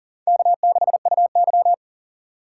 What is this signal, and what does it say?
Morse code, heard as a keyed tone, about 40 wpm, 700 Hz. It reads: K6UY
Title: USA Call Signs - 3x Character Spacing - 40wpm